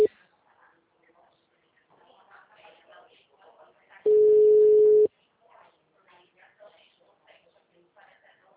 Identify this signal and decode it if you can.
Ringback tone (Europe, 425 Hz), with speech.
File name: out- -2030- - - .5249